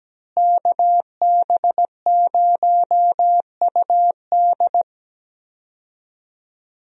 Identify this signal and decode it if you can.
Morse code, heard as a keyed tone, about 17 wpm, 700 Hz. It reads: KB0UD
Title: USA Call Signs - 5 Characters - Repeated 3x - 17wpm